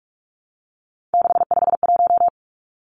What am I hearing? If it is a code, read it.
Morse code, heard as a keyed tone, about 44 wpm, 700 Hz. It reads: 651
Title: Numbers - 3-Digit 45wpm